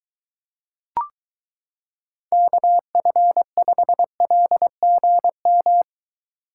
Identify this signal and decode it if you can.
Morse code, heard as a keyed tone, about 23 wpm, 700 Hz. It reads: KF5LGM